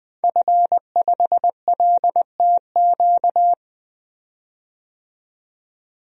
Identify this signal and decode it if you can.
Morse code, heard as a keyed tone, about 20 wpm, 700 Hz. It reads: F5LTQ
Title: DXCC-100 Call Signs - 5 Characters - 20wpm